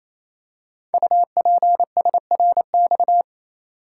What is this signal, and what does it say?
Morse code, heard as a keyed tone, about 29 wpm, 700 Hz. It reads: UPSRX
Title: Code Groups - 5 Letters - 28wpm